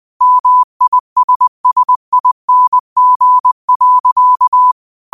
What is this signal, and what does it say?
Morse code, heard as a keyed tone, about 20 wpm, 1000 Hz. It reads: MISSING.
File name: qotdfiles / 20wpm